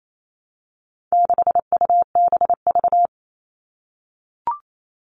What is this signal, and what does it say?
Morse code, heard as a keyed tone, about 28 wpm, 700 Hz. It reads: 6UBV
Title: All Cut Numbers - 4-Digit 28wpm